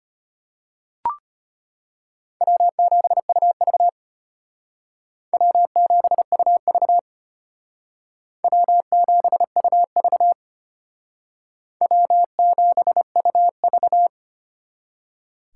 Morse code, heard as a keyed tone, about 32 wpm, 700 Hz. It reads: W7UV W7UV W7UV W7UV